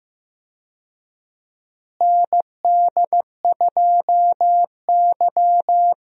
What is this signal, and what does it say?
Morse code, heard as a keyed tone, about 15 wpm, 700 Hz. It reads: ND2Y
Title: USA Call Signs - 4 Characters - Repeated 2x - 15wpm